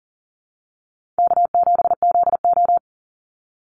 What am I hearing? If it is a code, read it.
Morse code, heard as a keyed tone, about 40 wpm, 700 Hz. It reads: K7ZO
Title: NA SOTA Chaser Callsigns 40wpm